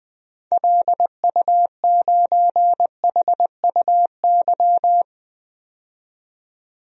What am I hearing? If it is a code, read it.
Morse code, heard as a keyed tone, about 20 wpm, 700 Hz. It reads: LU9HUY
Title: DXCC-100 Call Signs - 6 Characters - 20wpm